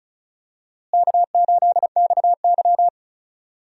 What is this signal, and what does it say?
Morse code, heard as a keyed tone, about 36 wpm, 700 Hz. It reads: K8XY
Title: USA Call Signs - 4 Characters - Repeated 3x - 35wpm